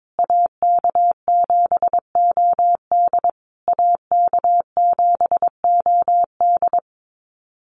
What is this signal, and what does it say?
Morse code, heard as a keyed tone, about 22 wpm, 700 Hz. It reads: AK7OD AK7OD
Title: USA Call Signs Repeated 3x 22wpm